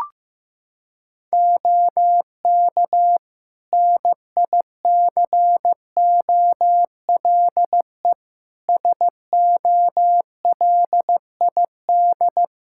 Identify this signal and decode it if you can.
Morse code, heard as a keyed tone, about 15 wpm, 700 Hz. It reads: OK NICOLE SOLID
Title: Practice QSO - Exchange 3 - More Info - 15wpm